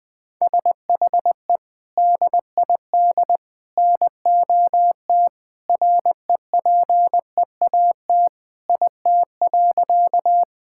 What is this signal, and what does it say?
Morse code, heard as a keyed tone, about 21 wpm, 700 Hz. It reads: SHE DID NOT REPEAT IT.
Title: Sentences from Top 1600 Words 20wpm